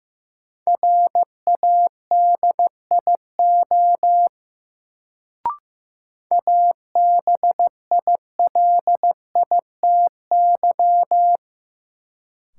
Morse code, heard as a keyed tone, about 15 wpm, 700 Hz. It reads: RADIO ABILITY
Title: Sentences from Top 1000 Words - Review Part 3 - New Word Drill - 15wpm